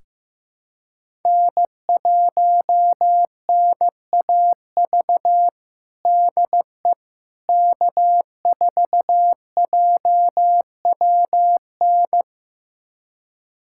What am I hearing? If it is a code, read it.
Morse code, heard as a keyed tone, about 16 wpm, 700 Hz. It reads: N1NAV DE K4JWN
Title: USA Callsign DE Callsign - 5 Characters - 15wpm